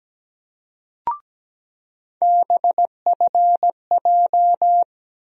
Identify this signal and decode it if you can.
Morse code, heard as a keyed tone, about 17 wpm, 700 Hz. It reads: BFJ
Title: USA Call Sign Postfixes - 3 Letters 17wpm